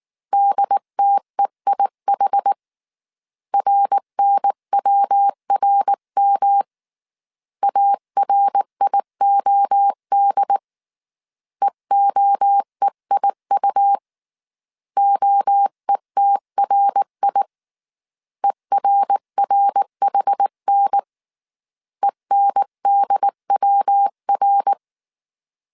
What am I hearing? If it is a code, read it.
Morse code, heard as a keyed tone, about 19 wpm, 800 Hz. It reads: DTEIH RNWRM ARIOD EOEIU OETRI ERRHN ENDWR